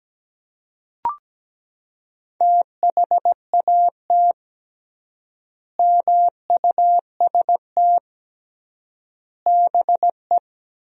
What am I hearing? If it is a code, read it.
Morse code, heard as a keyed tone, about 17 wpm, 700 Hz. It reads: THAT MUST BE
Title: 3x Word-Spacing - Sentences from Top 400 Words - 17wpm